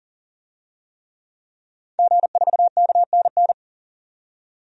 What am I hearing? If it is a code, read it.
Morse code, heard as a keyed tone, about 40 wpm, 700 Hz. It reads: G4KNN
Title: DXCC-100 Call Signs - 5 Characters - 40wpm